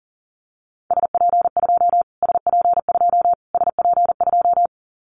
Morse code, heard as a keyed tone, about 40 wpm, 700 Hz. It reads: SP2 SP2 SP2
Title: Callsign Prefixes for Top 25 DXCC - Repeated 3x - 40wpm